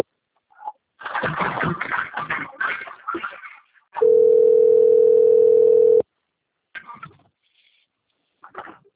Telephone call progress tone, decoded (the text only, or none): ringback tone